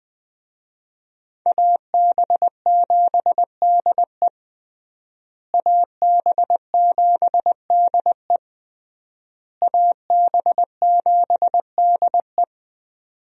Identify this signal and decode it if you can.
Morse code, heard as a keyed tone, about 21 wpm, 700 Hz. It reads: AB7DE AB7DE AB7DE
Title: USA Call Signs - 5 Characters - Repeated 3x - 20wpm